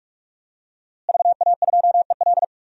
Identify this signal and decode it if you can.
Morse code, heard as a keyed tone, about 46 wpm, 700 Hz. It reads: VA2EL
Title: DXCC-100 Call Signs - 5 Characters - Repeated 2x - 45wpm